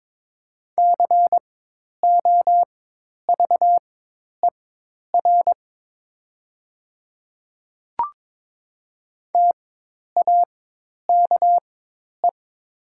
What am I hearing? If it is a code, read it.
Morse code, heard as a keyed tone, about 22 wpm, spaced as for 10 wpm, 700 Hz. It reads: COVER TAKE
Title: 4x Character Spacing - Top 500 Words - 22wpm